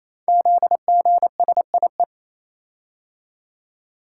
Morse code, heard as a keyed tone, about 27 wpm, 700 Hz. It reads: ZGSIE
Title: Code Groups - 5 Letters - 28wpm